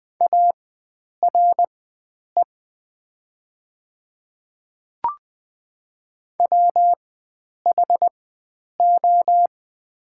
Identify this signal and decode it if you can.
Morse code, heard as a keyed tone, about 20 wpm, spaced as for 9 wpm, 700 Hz. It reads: ARE WHO
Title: 4x Character Spacing - Top 300 Words - 20wpm